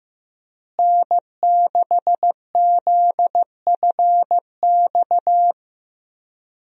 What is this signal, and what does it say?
Morse code, heard as a keyed tone, about 15 wpm, 700 Hz. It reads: N6ZFX
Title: USA Call Signs - 5 Characters - Repeated 3x - 15wpm